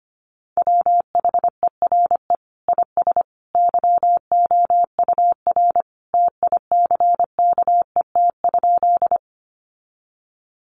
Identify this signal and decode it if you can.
Morse code, heard as a keyed tone, about 25 wpm, 700 Hz. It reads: WHERE IS YOUR TICKET?
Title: Sentences from Top 1400 Words 25wpm